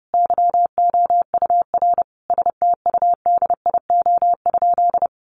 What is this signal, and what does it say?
Morse code, heard as a keyed tone, about 30 wpm, 700 Hz. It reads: YOUR STUDIO?
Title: Sentences from Top 1200 Words 30wpm